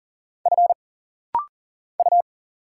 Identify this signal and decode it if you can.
Morse code, heard as a keyed tone, about 40 wpm, 700 Hz. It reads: F U
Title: Single Letters - Rapid-Fire 40wpm